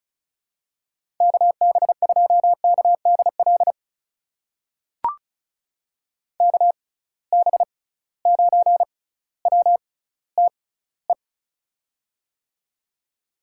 Morse code, heard as a keyed tone, about 36 wpm, spaced as for 18 wpm, 700 Hz. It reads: KB2KDL KB9WTE
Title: USA Call Signs - 6 Characters - 6x Character Spacing - 35wpm